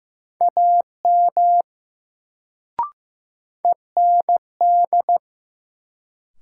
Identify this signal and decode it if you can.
Morse code, heard as a keyed tone, about 16 wpm, 700 Hz. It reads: AM END
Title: Sets of 2 Words - New Word Drill - 15wpm